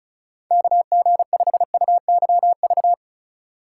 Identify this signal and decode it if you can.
Morse code, heard as a keyed tone, about 37 wpm, 700 Hz. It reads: KG5UYV
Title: Speed Racing - USA 2x3 Call Signs 30wpm